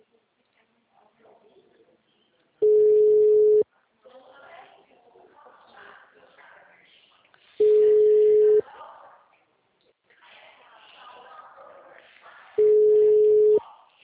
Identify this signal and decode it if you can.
Ringback tone (Europe, 425 Hz), with speech.